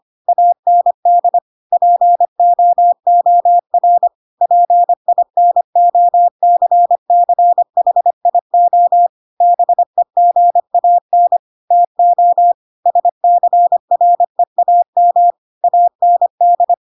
Morse code, heard as a keyed tone, about 25 wpm, 700 Hz. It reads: AND POOR PINOCCHIO BEGAN TO SCREAM AND
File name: Pinocchio / Pinocchio ch7 25 25 ik1hge